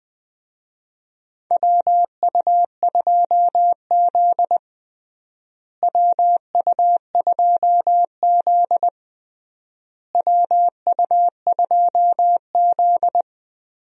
Morse code, heard as a keyed tone, about 20 wpm, 700 Hz. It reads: WU2Z WU2Z WU2Z
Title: USA Call Signs - 4 Characters - Repeated 3x - 20wpm